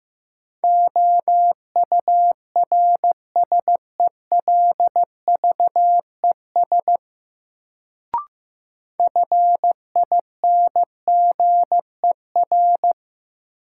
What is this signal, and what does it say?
Morse code, heard as a keyed tone, about 15 wpm, 700 Hz. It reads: OURSELVES FINGER
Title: Sentences from Top 1600 Words - New Word Drill - 15wpm